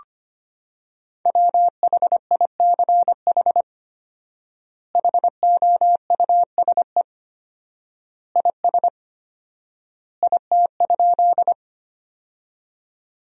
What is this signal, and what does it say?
Morse code, heard as a keyed tone, about 25 wpm, 700 Hz. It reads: WHICH HOUSE IS IT?
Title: 4x Word-Spacing - Sentences from Top 200 Words - 25wpm